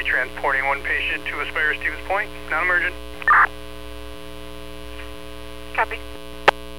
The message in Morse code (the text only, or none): none